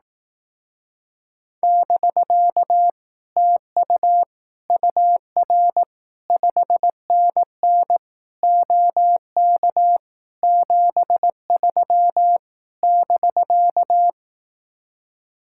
Morse code, heard as a keyed tone, about 18 wpm, 700 Hz. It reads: <BK> TU UR 5NN OK 73 <BK>
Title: Speed-Racing Common QSO Phrases Encore - 15wpm